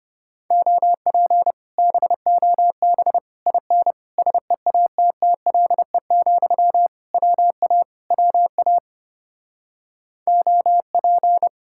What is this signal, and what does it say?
Morse code, heard as a keyed tone, about 33 wpm, 700 Hz. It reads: OP BOB IN SEATTLE, WA WA OP